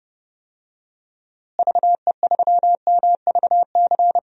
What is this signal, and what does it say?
Morse code, heard as a keyed tone, about 30 wpm, 700 Hz. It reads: VE3MVC
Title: DXCC-100 Call Signs - 6 Characters - Repeated 3x - 30wpm